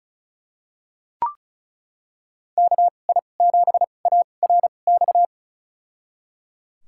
Morse code, heard as a keyed tone, about 36 wpm, spaced as for 25 wpm, 700 Hz. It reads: KI7ARX
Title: USA Call Signs - 6 Characters - 2x Character Spacing - 35wpm